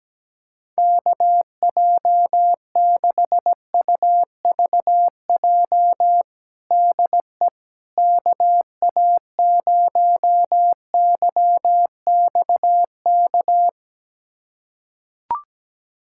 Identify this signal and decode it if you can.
Morse code, heard as a keyed tone, about 17 wpm, 700 Hz. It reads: KJ6UVJ DE KA0YXK
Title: USA Callsign DE Callsign - 6 Characters - 17wpm